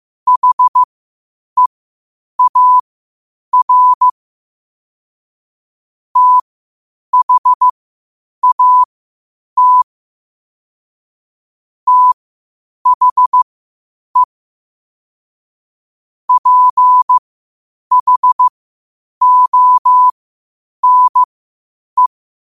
Morse code, heard as a keyed tone, about 15 wpm, spaced as for 8 wpm, 1000 Hz. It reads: HEAR THAT THE PHONE
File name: qotdfiles / 8wpm